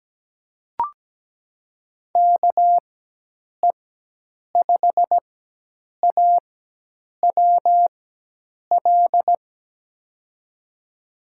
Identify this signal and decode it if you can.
Morse code, heard as a keyed tone, about 17 wpm, spaced as for 8 wpm, 700 Hz. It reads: KE5AWL
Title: USA Call Signs - 6 Characters - 4x Character Spacing - 17wpm